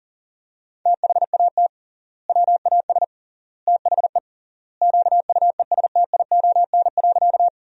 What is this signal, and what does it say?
Morse code, heard as a keyed tone, about 40 wpm, 700 Hz. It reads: THAT WAS THE QUESTION.
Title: 3x Word-Spacing - Sentences from Top 400 Words - 40wpm